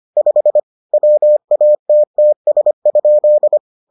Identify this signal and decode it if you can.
Morse code, heard as a keyed tone, about 25 wpm, 600 Hz. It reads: 5 WATTS?